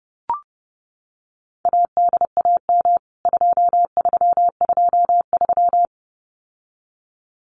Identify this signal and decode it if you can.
Morse code, heard as a keyed tone, about 30 wpm, 700 Hz. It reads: ADAM 2323
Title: CWT Exchange 30wpm